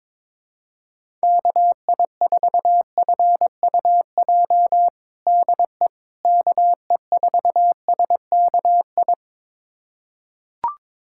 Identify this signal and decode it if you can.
Morse code, heard as a keyed tone, about 22 wpm, 700 Hz. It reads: KI4FUJ DE KE4SKI
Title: USA Callsign DE Callsign - 6 Characters - 22wpm